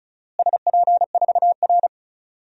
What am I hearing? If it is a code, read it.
Morse code, heard as a keyed tone, about 36 wpm, 700 Hz. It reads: SP4R
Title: DXCC-100 Call Signs - 4 Characters - Repeated 2x - 35wpm